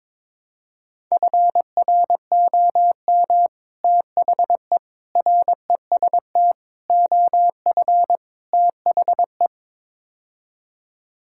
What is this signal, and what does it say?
Morse code, heard as a keyed tone, about 22 wpm, 700 Hz. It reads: FROM THE REST OF THE
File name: practice / 22wpm / Build A Phrase - 5-Word Phrases - 22wpm